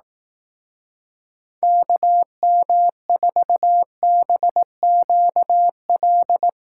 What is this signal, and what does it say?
Morse code, heard as a keyed tone, about 18 wpm, 700 Hz. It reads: KM4BQL
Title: Speed Racing - USA 2x3 Call Signs 15wpm